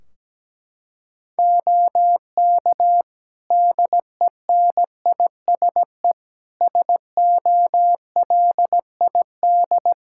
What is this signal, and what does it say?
Morse code, heard as a keyed tone, about 17 wpm, 700 Hz. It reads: OK DENISE SOLID
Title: K1USN SST QSO Practice 17wpm